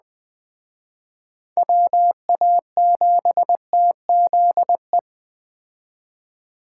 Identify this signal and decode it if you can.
Morse code, heard as a keyed tone, about 21 wpm, 700 Hz. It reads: WA7TZE
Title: Speed Racing - USA Call Signs 17wpm